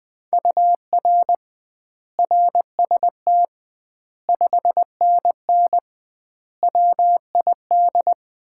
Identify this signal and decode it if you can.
Morse code, heard as a keyed tone, about 20 wpm, 700 Hz. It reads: UR RST 5NN WID
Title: Common QSO Phrases - 2x Word-Spacing - 20wpm